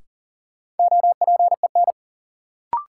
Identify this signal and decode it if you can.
Morse code, heard as a keyed tone, about 40 wpm, 700 Hz. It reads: OPEN
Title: Sentences from Top 300 Words - New Word Drill - 40wpm